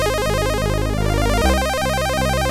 There is no code.